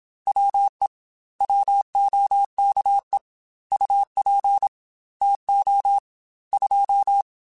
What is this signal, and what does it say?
Morse code, heard as a keyed tone, about 26 wpm, 800 Hz. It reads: WE WOKE UP TO 2
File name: Weather Sets / Weather Sets-mp3s / Weather 3 17 @25